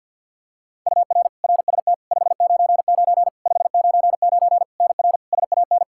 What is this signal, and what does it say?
Morse code, heard as a keyed tone, about 50 wpm, 700 Hz. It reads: UR RST 599 599 NR SAN
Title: Practice QSO - Exchange 1 - Beginning - 50wpm